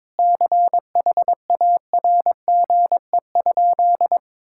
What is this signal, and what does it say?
Morse code, heard as a keyed tone, about 22 wpm, 700 Hz. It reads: CHARGE?
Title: Sentences from Top 1000 Words - Review Part 6 - 22wpm